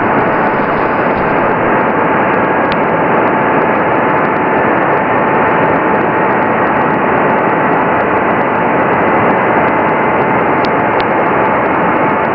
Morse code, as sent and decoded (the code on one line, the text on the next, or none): none